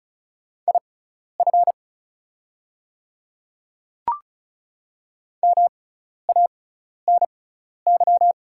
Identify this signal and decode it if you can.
Morse code, heard as a keyed tone, about 36 wpm, spaced as for 12 wpm, 700 Hz. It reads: IF MANY
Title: 6x Character Spacing - Top 200 Words - 35wpm